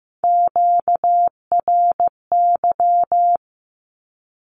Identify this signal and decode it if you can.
Morse code, heard as a keyed tone, about 15 wpm, 700 Hz. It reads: QRY
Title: Q-Codes 15wpm